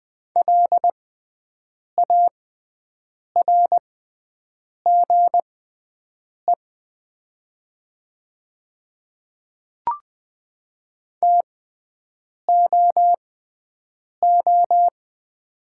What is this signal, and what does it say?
Morse code, heard as a keyed tone, about 20 wpm, spaced as for 7 wpm, 700 Hz. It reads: LARGE TOO
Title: 6x Character Spacing - Top 200 Words - 20wpm